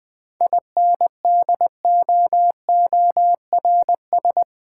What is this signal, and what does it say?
Morse code, heard as a keyed tone, about 20 wpm, 700 Hz. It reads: INDOORS
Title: 7 Letter Words - First 3 Letters Given - 20wpm